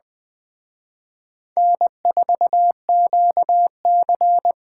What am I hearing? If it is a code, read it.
Morse code, heard as a keyed tone, about 20 wpm, 700 Hz. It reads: N4QC
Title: Speed Racing - USA 1x2 Call Signs 17wpm